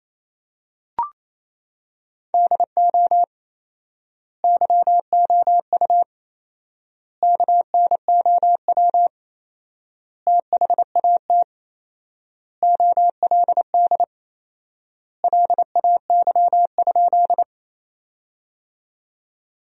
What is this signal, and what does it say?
Morse code, heard as a keyed tone, about 28 wpm, 700 Hz. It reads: DO YOU KNOW THAT OLD LAY?
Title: 4x Word-Spacing - Sentences from Top 400 Words - 28wpm